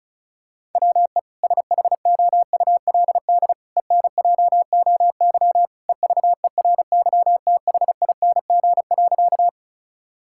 Morse code, heard as a keyed tone, about 36 wpm, 700 Hz. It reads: WE SHOULD ENJOY EVERYTHING.
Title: Sentences from Top 1000 Words - Review Part 5 - 35wpm